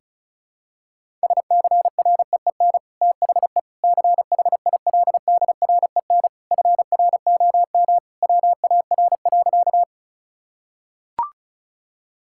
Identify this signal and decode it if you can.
Morse code, heard as a keyed tone, about 36 wpm, 700 Hz. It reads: SCREEN THE CHILDREN FROM WAR.
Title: Sentences from Top 1000 Words - Review Part 6 - 35wpm